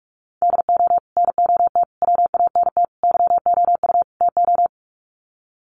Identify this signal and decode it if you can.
Morse code, heard as a keyed tone, about 44 wpm, 700 Hz. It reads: DO NOT WANT YOU TO
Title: Sets of 5 Words 45wpm